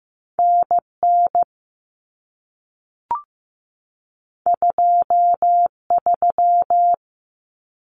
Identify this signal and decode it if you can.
Morse code, heard as a keyed tone, about 15 wpm, 700 Hz. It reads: NN 23
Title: Common Cut Numbers - 2-Digit 15wpm